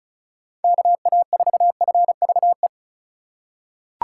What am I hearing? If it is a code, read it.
Morse code, heard as a keyed tone, about 36 wpm, 700 Hz. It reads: KA4FVE
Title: USA Call Signs - 6 Characters - 3x Character Spacing - 35wpm